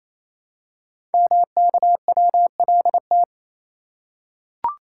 Morse code, heard as a keyed tone, about 27 wpm, 700 Hz. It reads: MKWLT
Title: Code Groups - 5 Letter:Numbers - 28wpm